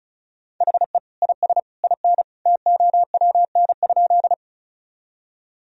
Morse code, heard as a keyed tone, about 36 wpm, 700 Hz. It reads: HE IS IN TOWN?
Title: Sentences from Top 1000 Words - Review Part 2 - 35wpm